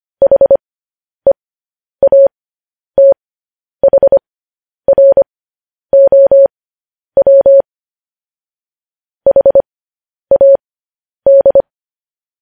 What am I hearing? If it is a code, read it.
Morse code, heard as a keyed tone, about 25 wpm, spaced as for 10 wpm, 550 Hz. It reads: HEATHROW HAD
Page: news-headlines-in-morse-code-at-10-wpm